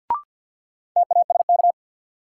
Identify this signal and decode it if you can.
Morse code, heard as a keyed tone, about 52 wpm, 700 Hz. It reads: TASK
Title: Sentences from Top 1000 Words - Review Part 7 - New Word Drill - 50wpm